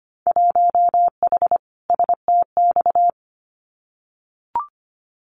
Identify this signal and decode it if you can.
Morse code, heard as a keyed tone, about 25 wpm, 700 Hz. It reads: 1H STX
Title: Winter Field Day Exchanges 25wpm